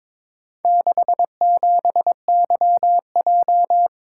Morse code, heard as a keyed tone, about 22 wpm, 690 Hz. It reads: 67YJ
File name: practice / 22wpm / Four Letter-Number Combinations - 22wpm